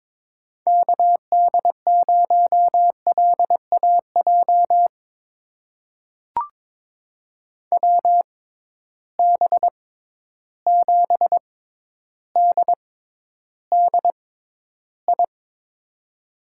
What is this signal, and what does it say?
Morse code, heard as a keyed tone, about 22 wpm, spaced as for 11 wpm, 700 Hz. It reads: KD0LAJ WB7DDI
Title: USA Call Signs - 6 Characters - 6x Character Spacing - 22wpm